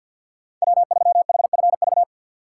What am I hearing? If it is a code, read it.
Morse code, heard as a keyed tone, about 52 wpm, 700 Hz. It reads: W3HFV
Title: USA Call Signs - 6x Character Spacing - 50wpm